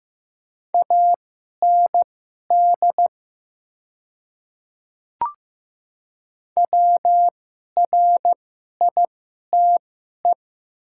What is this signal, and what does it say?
Morse code, heard as a keyed tone, about 15 wpm, spaced as for 11 wpm, 700 Hz. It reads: AND WRITE
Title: 2x Character Spacing - Top 100 Words - 15wpm